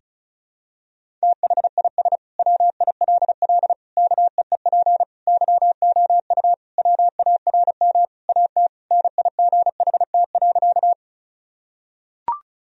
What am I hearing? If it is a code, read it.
Morse code, heard as a keyed tone, about 34 wpm, 700 Hz. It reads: THIS WILL KEEP YOU WARM AT NIGHT.
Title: Sentences from Top 1000 Words - Review Part 2 - 35wpm